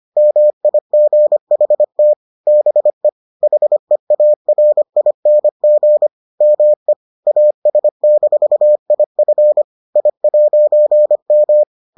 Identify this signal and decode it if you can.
Morse code, heard as a keyed tone, about 25 wpm, 600 Hz. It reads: MIGHT BE HEARING ME AS-IF I'M